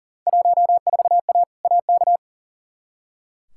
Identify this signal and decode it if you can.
Morse code, heard as a keyed tone, about 37 wpm, 700 Hz. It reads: 14A AK
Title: Field Day Exchanges Encore - 40wpm